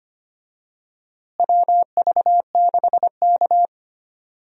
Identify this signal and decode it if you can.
Morse code, heard as a keyed tone, about 25 wpm, 700 Hz. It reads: WV6K
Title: USA Call Signs - 4 Characters - Repeated 2x - 25wpm